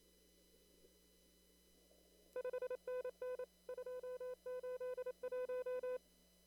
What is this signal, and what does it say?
Morse code, heard as a keyed tone, about 26 wpm, 500 Hz. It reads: 5NN 281
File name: CW / kt5j wpx cw-op-k5pi